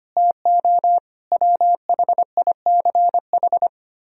Morse code, heard as a keyed tone, about 32 wpm, 700 Hz. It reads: TO WHICH